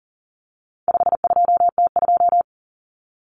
Morse code, heard as a keyed tone, about 40 wpm, 700 Hz. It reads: 52T2
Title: Common Cut Numbers - 4-Digit 40wpm